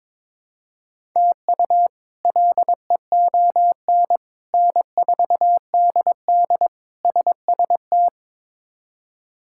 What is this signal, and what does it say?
Morse code, heard as a keyed tone, about 22 wpm, 700 Hz. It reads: TU LEON N4DD SST
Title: K1USN SST QSO Practice - 22wpm